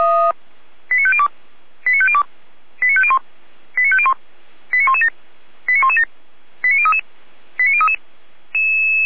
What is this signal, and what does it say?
Five-tone selective call (ZVEI), heard as quick runs of five tones, tones of about 70 ms. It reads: 89502 89502 89501 89501 89117 89117 89033 89033
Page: /455/SDR/ZVEI/89115/